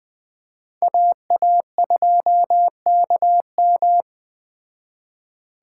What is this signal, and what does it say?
Morse code, heard as a keyed tone, about 20 wpm, 700 Hz. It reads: AA2KM
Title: Speed Racing - USA Call Signs 17wpm